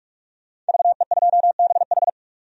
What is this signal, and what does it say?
Morse code, heard as a keyed tone, about 45 wpm, 700 Hz. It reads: VE2BH
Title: DXCC-100 Call Signs - 5 Characters - 45wpm